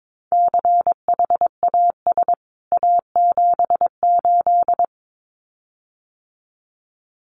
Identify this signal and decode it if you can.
Morse code, heard as a keyed tone, about 22 wpm, 700 Hz. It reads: CHAS A78
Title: CWT Exchange 22wpm